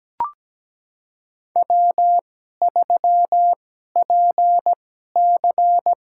Morse code, heard as a keyed tone, about 17 wpm, spaced as for 12 wpm, 700 Hz. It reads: W3PC
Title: USA Call Signs - 4 Characters - 2x Character Spacing - 17wpm